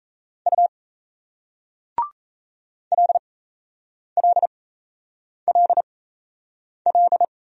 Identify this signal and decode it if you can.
Morse code, heard as a keyed tone, about 38 wpm, spaced as for 8 wpm, 700 Hz. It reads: U LLLL